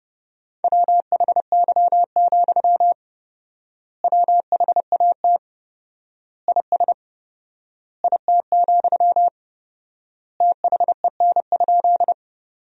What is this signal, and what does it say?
Morse code, heard as a keyed tone, about 30 wpm, 700 Hz. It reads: WHY, WHAT IS IT, THEN?
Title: 4x Word-Spacing - Sentences from Top 200 Words - 30wpm